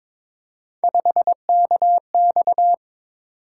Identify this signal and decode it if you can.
Morse code, heard as a keyed tone, about 22 wpm, 700 Hz. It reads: 5KX
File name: practice / 22wpm / Three Letter-Number Combinations - 22wpm